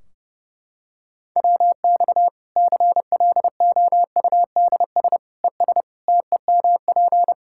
Morse code, heard as a keyed tone, about 32 wpm, 700 Hz. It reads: WX CLOUDS ES TEMP
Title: Practice QSO - Exchange 2 - Info - 30wpm